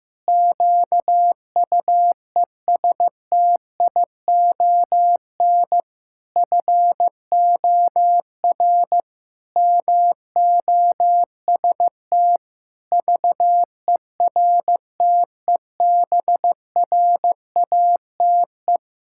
Morse code, heard as a keyed tone, about 15 wpm, 700 Hz. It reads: QUESTION FOR MOST VERTEBRATE
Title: Pangram Challenge v2 - 15wpm